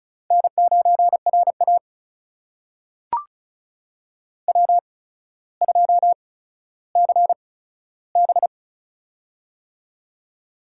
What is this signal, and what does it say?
Morse code, heard as a keyed tone, about 36 wpm, spaced as for 15 wpm, 700 Hz. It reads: N9RA W2CB
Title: USA Call Signs - 8x Character Spacing - 35wpm